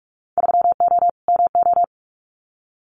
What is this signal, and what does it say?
Morse code, heard as a keyed tone, about 46 wpm, 700 Hz. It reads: 3O MO
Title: Winter Field Day Exchanges Encore 45wpm